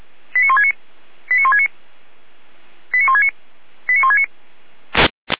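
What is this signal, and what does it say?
Five-tone selective call (ZVEI), heard as quick runs of five tones, tones of about 70 ms. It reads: 89169 89169 89169 89169
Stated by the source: ZVEI five-tone call received by SDR.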